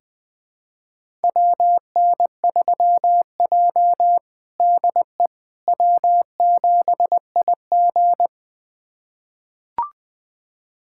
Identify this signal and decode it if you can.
Morse code, heard as a keyed tone, about 20 wpm, 700 Hz. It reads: WN3J DE W7IG
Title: USA Callsign DE Callsign - 4 Characters - 20wpm